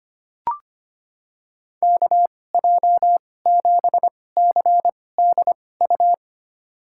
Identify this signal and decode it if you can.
Morse code, heard as a keyed tone, about 25 wpm, spaced as for 18 wpm, 700 Hz. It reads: KJ7CDU
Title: USA Call Signs - 6 Characters - 2x Character Spacing - 25wpm